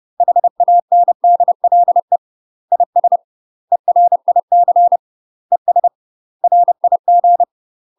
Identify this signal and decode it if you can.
Morse code, heard as a keyed tone, about 30 wpm, 700 Hz. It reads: HANDLE IS ERIC ES RIG
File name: UK MASTER QSOs TRAINING / 30WPM / UK MASTER QSOs 0024